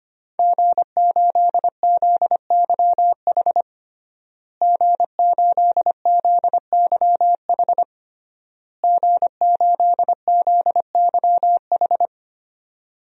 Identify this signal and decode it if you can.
Morse code, heard as a keyed tone, about 25 wpm, 700 Hz. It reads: G8ZYH G8ZYH G8ZYH
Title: DXCC-100 Call Signs - 5 Characters - Repeated 3x - 25wpm